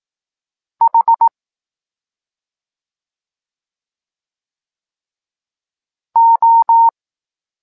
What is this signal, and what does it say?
Morse code, heard as a keyed tone, about 18 wpm, 920 Hz. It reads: H O
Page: Tekst 14 – doskonalenie odbioru poznanych liter